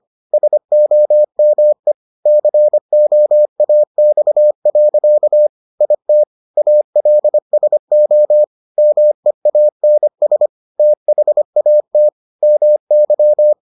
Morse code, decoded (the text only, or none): SOME COAX. IT ALSO MEANS THAT MY